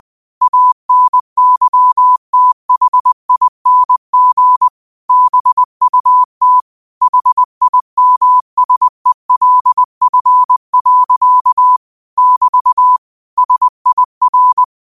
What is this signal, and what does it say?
Morse code, heard as a keyed tone, about 20 wpm, 1000 Hz. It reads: ANYTHING BUT HIMSELF. <BT> SIR
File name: qotdfiles / 20wpm